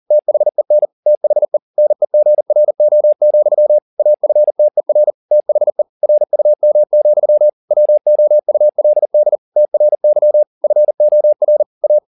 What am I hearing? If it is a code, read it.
Morse code, heard as a keyed tone, about 40 wpm, 600 Hz. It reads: THEN THE NEGRO, AFTER THE RUM, WOULD TRY FOR A
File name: the old man and the sea 2 40wpm